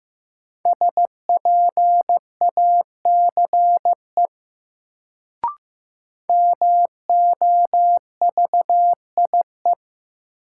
Morse code, heard as a keyed tone, about 16 wpm, 700 Hz. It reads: SPACE MOVIE
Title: Sentences from Top 1000 Words - Review Part 5 - New Word Drill - 15wpm